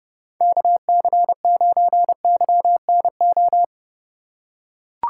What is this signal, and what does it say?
Morse code, heard as a keyed tone, about 28 wpm, 700 Hz. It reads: KC9YNO